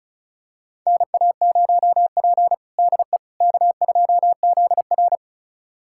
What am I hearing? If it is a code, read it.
Morse code, heard as a keyed tone, about 36 wpm, 700 Hz. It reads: NA0P DE K2ZR
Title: USA Callsign DE Callsign - 4 Characters - 35wpm